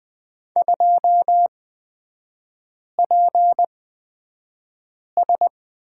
Morse code, heard as a keyed tone, about 20 wpm, spaced as for 5 wpm, 700 Hz. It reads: 2PS
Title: 236 Lesson 38e - Letter q - Three Characters 20wpm